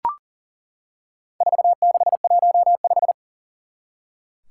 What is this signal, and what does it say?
Morse code, heard as a keyed tone, about 40 wpm, 690 Hz. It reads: 4615
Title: Four Challenging Letter-Number Combinations 40wpm